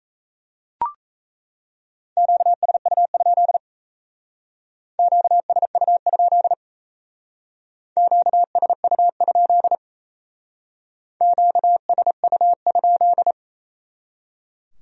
Morse code, heard as a keyed tone, about 35 wpm, 700 Hz. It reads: QSU? QSU? QSU? QSU?
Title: Speed-Racing Q-Codes - 28wpm